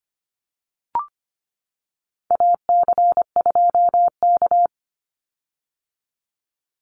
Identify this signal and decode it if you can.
Morse code, heard as a keyed tone, about 25 wpm, 700 Hz. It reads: AC2K
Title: Call Signs - 25wpm